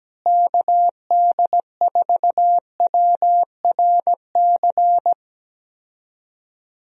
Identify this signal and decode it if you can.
Morse code, heard as a keyed tone, about 17 wpm, 700 Hz. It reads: KD4WRC
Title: USA Call Signs - 6 Characters - Repeated 3x - 17wpm